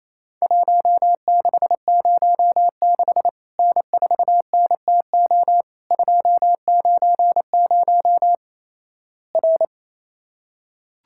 Morse code, heard as a keyed tone, about 28 wpm, 700 Hz. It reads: 1606 N4NTO 290 R
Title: Worked All Europe DX Contest - QTC Exchange - 28wpm